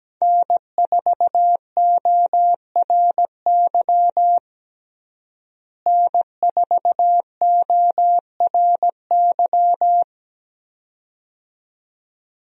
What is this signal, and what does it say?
Morse code, heard as a keyed tone, about 17 wpm, 700 Hz. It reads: N4ORY N4ORY
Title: USA Call Signs - 5 Characters - Repeated 2x - 17wpm